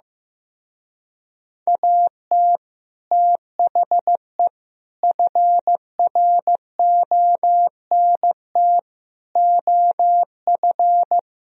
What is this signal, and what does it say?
Morse code, heard as a keyed tone, about 15 wpm, 700 Hz. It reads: AT THE FRONT OF
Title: Speed-Racing - Sets of 5 Words 15wpm